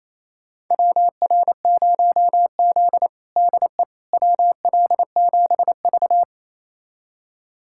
Morse code, heard as a keyed tone, about 28 wpm, 700 Hz. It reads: WR0Z DE WL7V
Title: USA Callsign DE Callsign - 4 Characters - 28wpm